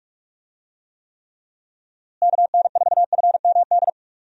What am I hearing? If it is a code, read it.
Morse code, heard as a keyed tone, about 46 wpm, 700 Hz. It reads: KN4FMD
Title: USA Call Signs - 6 Characters - Repeated 3x - 45wpm